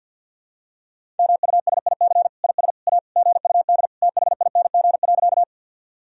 Morse code, heard as a keyed tone, about 52 wpm, 700 Hz. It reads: MUSIC IS A GUD THING.
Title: Sentences from Top 300 Words and Abbreviations 50wpm